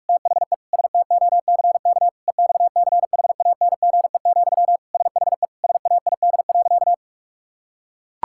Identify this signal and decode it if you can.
Morse code, heard as a keyed tone, about 41 wpm, 700 Hz. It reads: THE STOCK EXCHANGE, SHE SAID.
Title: Sentences from Top 1000 Words - Review Part 6 - 45wpm